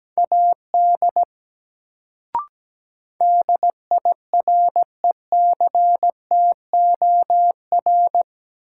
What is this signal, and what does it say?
Morse code, heard as a keyed tone, about 17 wpm, 700 Hz. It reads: AD DIRECTOR
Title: Sentences from Top 1300 Words - New Word Drill Alt - 17wpm